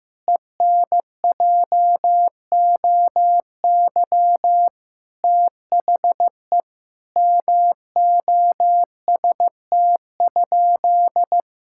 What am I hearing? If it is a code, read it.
Morse code, heard as a keyed tone, about 15 wpm, 700 Hz. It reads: ENJOY THE MOST?
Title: Sentences from Top 1000 Words - Review Part 5 - 15wpm